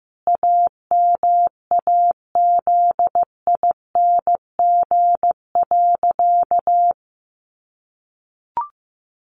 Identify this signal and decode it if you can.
Morse code, heard as a keyed tone, about 15 wpm, 700 Hz. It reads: AMAZING.
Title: Sentences from Top 1500 Words 15wpm